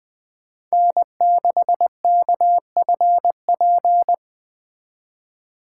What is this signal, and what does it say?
Morse code, heard as a keyed tone, about 21 wpm, 700 Hz. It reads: N6KFP